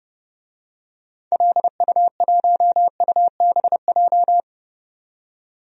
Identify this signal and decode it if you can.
Morse code, heard as a keyed tone, about 32 wpm, 700 Hz. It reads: LU1UBJ
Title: DXCC-100 Call Signs - 6 Characters - 30wpm